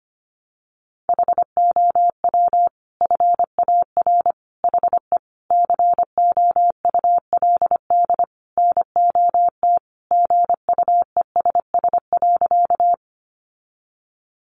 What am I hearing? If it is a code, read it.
Morse code, heard as a keyed tone, about 25 wpm, 700 Hz. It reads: HOW FAR HE COULD NOT GUESS.
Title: Sentences from Top 1500 Words 25wpm